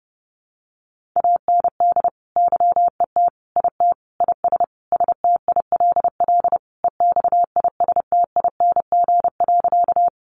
Morse code, heard as a keyed tone, about 30 wpm, 700 Hz. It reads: AND YET IT IS STILL EXISTING.
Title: Sentences from Top 1500 Words 30wpm